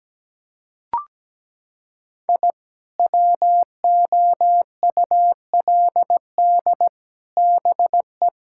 Morse code, heard as a keyed tone, about 17 wpm, 700 Hz. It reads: I WOULD BE
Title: Sentences from Top 1000 Words - Review Part 5 - 17wpm